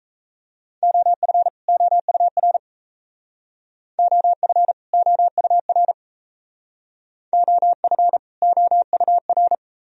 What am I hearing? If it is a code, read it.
Morse code, heard as a keyed tone, about 37 wpm, 700 Hz. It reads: OF OUR OF OUR OF OUR